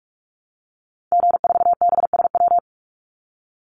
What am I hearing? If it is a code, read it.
Morse code, heard as a keyed tone, about 44 wpm, 700 Hz. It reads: G4BSW
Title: Call Signs Repeated 2x - 45wpm